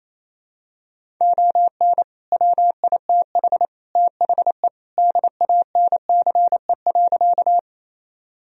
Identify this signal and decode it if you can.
Morse code, heard as a keyed tone, about 28 wpm, 700 Hz. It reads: ON WITH THE DANCE.
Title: Sentences from Top 1000 Words - Review Part 4 - 28wpm